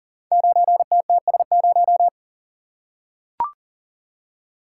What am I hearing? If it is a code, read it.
Morse code, heard as a keyed tone, about 43 wpm, 700 Hz. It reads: 9TTS0
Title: Code Groups - 5 Letter:Numbers - 40wpm